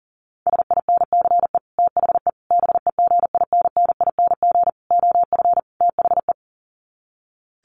Morse code, heard as a keyed tone, about 40 wpm, 700 Hz. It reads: SINCE THE BEGINNING OF THE
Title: Sets of 5 Words Encore2 40wpm